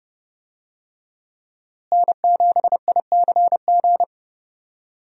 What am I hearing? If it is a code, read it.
Morse code, heard as a keyed tone, about 30 wpm, 700 Hz. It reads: N7ICG
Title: USA Call Signs - 5 Characters - Repeated 2x - 30wpm